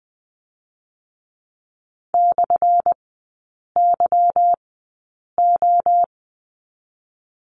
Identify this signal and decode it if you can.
Morse code, heard as a keyed tone, about 20 wpm, spaced as for 8 wpm, 700 Hz. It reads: /YO
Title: 198 Lesson 32e - Letter k - Three Characters 20wpm